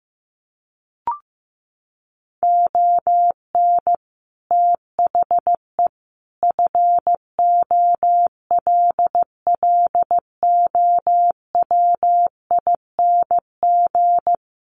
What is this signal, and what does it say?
Morse code, heard as a keyed tone, about 15 wpm, 700 Hz. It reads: ON THE FOLLOWING
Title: Sentences from Top 1400 Words 15wpm